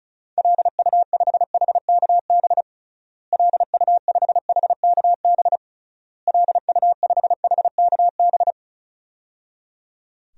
Morse code, heard as a keyed tone, about 34 wpm, 700 Hz. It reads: LU5HKB LU5HKB LU5HKB
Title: DXCC-100 Call Signs - 6 Characters - Repeated 3x - 35wpm